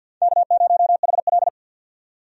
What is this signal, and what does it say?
Morse code, heard as a keyed tone, about 52 wpm, 700 Hz. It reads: K0HL
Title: USA Call Signs - 4 Characters - 50wpm